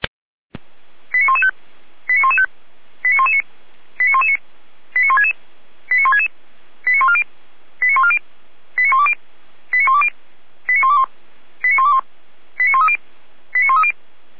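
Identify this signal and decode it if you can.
Five-tone selective call (ZVEI), heard as quick runs of five tones, tones of about 80 ms. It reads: 89115 89115 89119 89119 89166 89166 89140 89140 89120 89120 89121 89121 89130 89130
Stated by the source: Five-tone ZVEI selective call 89092, received on SDR.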